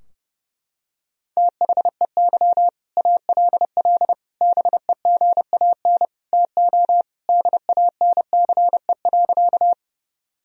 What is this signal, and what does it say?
Morse code, heard as a keyed tone, about 32 wpm, 700 Hz. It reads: THEY ALL BEGAN TO DANCE.
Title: Sentences from Top 1000 Words - Review Part 4 - 30wpm